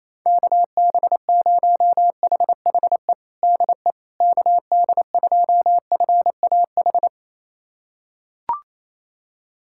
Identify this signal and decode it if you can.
Morse code, heard as a keyed tone, about 28 wpm, 700 Hz. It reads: KB0HHE DE KD2FAH
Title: USA Callsign DE Callsign 28wpm